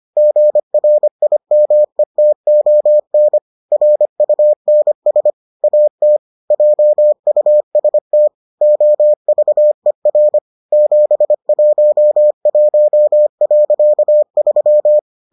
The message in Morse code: GRIMETON RUNS AT JUST OVER 711.3

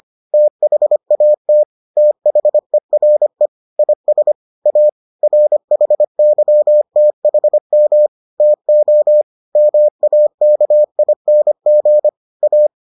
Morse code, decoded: THAT THERE IS A RHYTHM TO MAKING A

